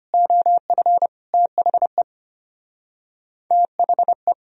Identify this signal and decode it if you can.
Morse code, heard as a keyed tone, about 24 wpm, 700 Hz. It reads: OF THE THE
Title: Speed-Racing - Sets of 4 Words 25wpm